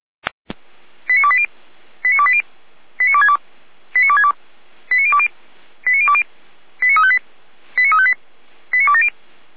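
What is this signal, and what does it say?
Five-tone selective call (ZVEI), heard as quick runs of five tones, tones of about 70 ms. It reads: 89280 89280 89262 89262 89020 89020 89358 89358 89270
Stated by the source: ZVEI five-tone selective call received by SDR.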